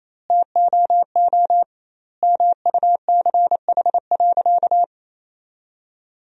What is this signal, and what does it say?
Morse code, heard as a keyed tone, about 28 wpm, 700 Hz. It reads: TOO MUCH.
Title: 2x Word-Spacing - Sentences from Top 300 Words - 28wpm